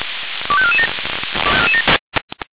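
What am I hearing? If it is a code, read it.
Five-tone selective call (ZVEI), heard as quick runs of five tones, tones of about 70 ms. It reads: 25447
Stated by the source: ZVEI five-tone selective call 25435 received on SDR.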